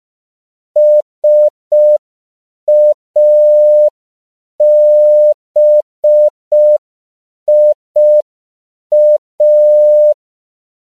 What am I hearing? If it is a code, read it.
Morse code, heard as a keyed tone, about 5 wpm, 600 Hz. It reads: SABIA
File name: CW / 5PPM